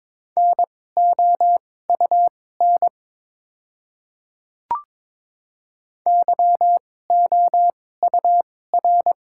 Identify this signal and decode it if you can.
Morse code, heard as a keyed tone, about 23 wpm, spaced as for 16 wpm, 700 Hz. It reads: NOUN YOUR
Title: 2x Character Spacing - Top 500 Words - 22wpm